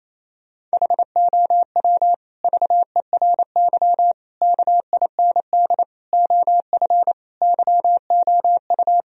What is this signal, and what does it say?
Morse code, heard as a keyed tone, about 28 wpm, 700 Hz. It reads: HOW VERY KIND OF YOU
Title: Sentences from Top 1000 Words - Review Part 1 - 28wpm